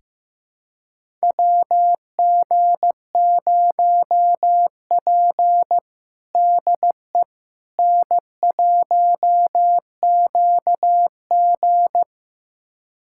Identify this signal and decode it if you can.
Morse code, heard as a keyed tone, about 15 wpm, 700 Hz. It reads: WG0P DE N1QG
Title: USA Callsign DE Callsign - 4 Characters - 15wpm